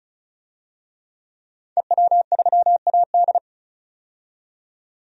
Morse code, heard as a keyed tone, about 36 wpm, 700 Hz. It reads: EW3AD
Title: DXCC-100 Call Signs - 5 Characters - 35wpm